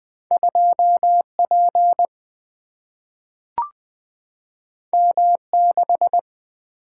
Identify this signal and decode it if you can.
Morse code, heard as a keyed tone, about 20 wpm, 700 Hz. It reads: 2 P M 6
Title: Single-Letter Single-Number 20wpm